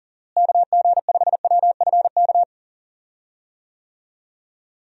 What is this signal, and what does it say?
Morse code, heard as a keyed tone, about 40 wpm, 700 Hz. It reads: KG5WFK
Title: USA Call Signs - 6 Characters - Repeated 2x - 40wpm